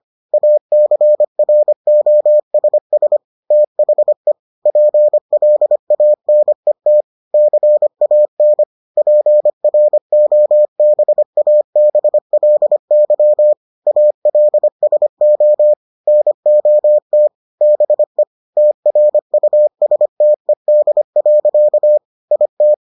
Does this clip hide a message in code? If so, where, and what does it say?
Yes: Morse code from 0.3 to 22.7 s, heard as a keyed tone, about 25 wpm, 600 Hz. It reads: ACROSS THE PLANET CAN PROBABLY ALSO NOT BE TRUSTED. IT